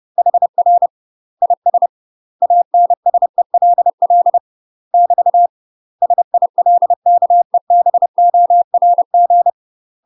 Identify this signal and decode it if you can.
Morse code, heard as a keyed tone, about 30 wpm, 700 Hz. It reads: HR IS ANSELL <BT> SILKEBORG
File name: UK MASTER QSOs TRAINING / 30WPM / UK MASTER QSOs 0014